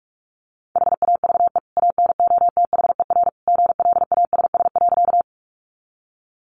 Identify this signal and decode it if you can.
Morse code, heard as a keyed tone, about 44 wpm, 700 Hz. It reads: HAVE ANOTHER GLASS.
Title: Sentences from Top 1300 Words 45wpm